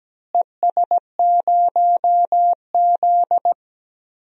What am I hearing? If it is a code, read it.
Morse code, heard as a keyed tone, about 17 wpm, 700 Hz. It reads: ES0Z
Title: DXCC-100 Call Signs - 4 Characters - Repeated 2x - 17wpm